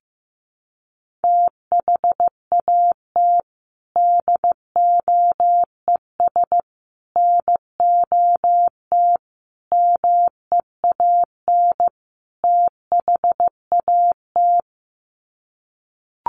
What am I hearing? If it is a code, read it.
Morse code, heard as a keyed tone, about 15 wpm, 700 Hz. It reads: THAT DOES NOT MEAN THAT
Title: Sets of 5 Words Encore 15wpm